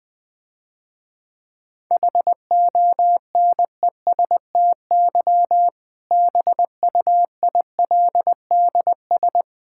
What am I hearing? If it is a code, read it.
Morse code, heard as a keyed tone, about 20 wpm, 700 Hz. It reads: HONESTY BUILDS
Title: Difficult Sentences with Hints - 20wpm